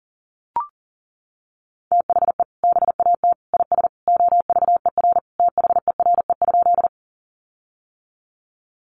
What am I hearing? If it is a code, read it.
Morse code, heard as a keyed tone, about 37 wpm, 700 Hz. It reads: THE BAT IS OVER THERE?
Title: Sentences from Top 1500 Words 40wpm